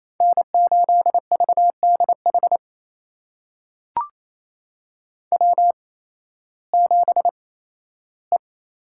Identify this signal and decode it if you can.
Morse code, heard as a keyed tone, about 28 wpm, spaced as for 15 wpm, 700 Hz. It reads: N8VDH W7E
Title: USA Call Signs - 5 Characters - 8x Character Spacing - 28wpm